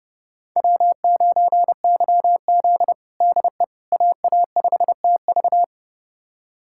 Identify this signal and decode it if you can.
Morse code, heard as a keyed tone, about 30 wpm, 700 Hz. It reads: W9YZ DE AA5TV
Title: USA Callsign DE Callsign 30wpm